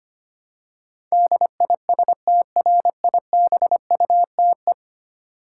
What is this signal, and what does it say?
Morse code, heard as a keyed tone, about 25 wpm, 700 Hz. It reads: DISTRIBUTE
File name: practice / 25wpm / Ten Letter Words - 25wpm